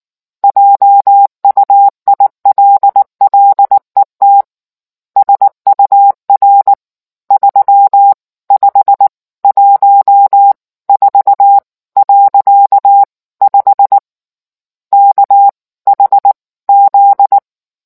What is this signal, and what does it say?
Morse code, heard as a keyed tone, about 19 wpm, spaced as for 16 wpm, 800 Hz. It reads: JUILLET SUR 3514.5 KHZ